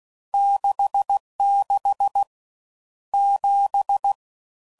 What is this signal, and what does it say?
Morse code, heard as a keyed tone, about 16 wpm, 800 Hz. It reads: 66 7
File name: Mixed / Misc / Numbs